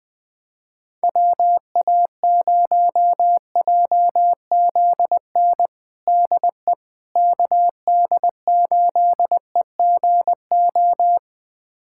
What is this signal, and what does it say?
Morse code, heard as a keyed tone, about 20 wpm, 700 Hz. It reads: WA0JZN DE KD8EGO
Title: USA Callsign DE Callsign 20wpm